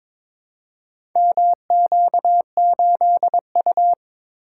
Morse code, heard as a keyed tone, about 22 wpm, 700 Hz. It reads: MQ8U
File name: practice / 22wpm / Four Letter-Number Combinations - 22wpm